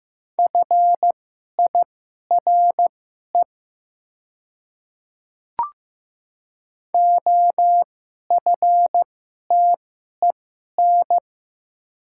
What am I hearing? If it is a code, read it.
Morse code, heard as a keyed tone, about 15 wpm, spaced as for 11 wpm, 700 Hz. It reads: FIRE OFTEN
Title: 2x Character Spacing - Top 500 Words - 15wpm